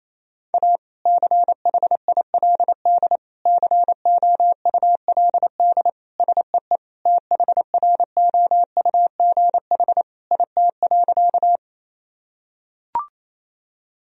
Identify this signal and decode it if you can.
Morse code, heard as a keyed tone, about 28 wpm, 700 Hz. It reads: A CHILD COULD SEE THROUGH IT.
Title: Sentences from Top 1000 Words - Review Part 1 - 28wpm